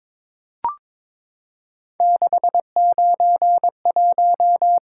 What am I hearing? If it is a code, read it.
Morse code, heard as a keyed tone, about 22 wpm, 700 Hz. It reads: 691
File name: practice / 22wpm / Three Letter-Number Combinations - 22wpm